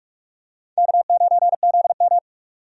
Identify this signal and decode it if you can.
Morse code, heard as a keyed tone, about 45 wpm, 700 Hz. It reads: K9ZM
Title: USA Call Signs - 4 Characters - 45wpm